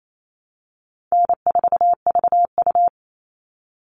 Morse code, heard as a keyed tone, about 28 wpm, 700 Hz. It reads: N4VU
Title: All Cut Numbers - 4-Digit 28wpm